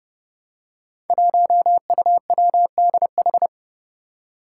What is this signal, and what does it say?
Morse code, heard as a keyed tone, about 29 wpm, 700 Hz. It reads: 1UWDH
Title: Five Challenging Letter-Number Combinations 30wpm